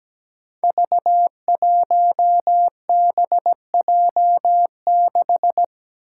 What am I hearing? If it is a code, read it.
Morse code, heard as a keyed tone, about 17 wpm, 700 Hz. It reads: V1BJ6
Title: Five Challenging Letter-Number Combinations 17wpm